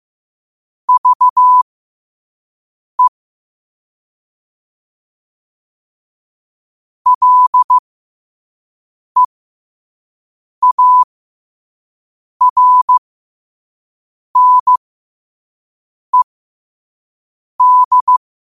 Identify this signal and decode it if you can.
Morse code, heard as a keyed tone, about 15 wpm, spaced as for 5 wpm, 1000 Hz. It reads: VE LEARNED